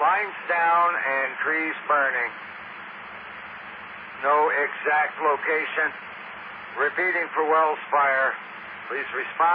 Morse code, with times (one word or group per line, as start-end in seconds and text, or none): none